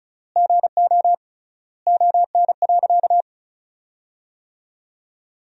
Morse code, heard as a keyed tone, about 34 wpm, 700 Hz. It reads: GO ON.
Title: 3x Word-Spacing - Sentences from Top 200 Words - 35wpm